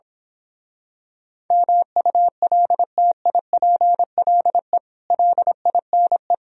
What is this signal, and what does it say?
Morse code, heard as a keyed tone, about 26 wpm, 700 Hz. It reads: MULTIPLE LINE
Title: Speed-Racing - Sets of 5 Words 22wpm